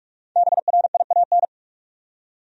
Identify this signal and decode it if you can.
Morse code, heard as a keyed tone, about 46 wpm, 700 Hz. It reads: BRIAN
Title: USA Common Names - 45wpm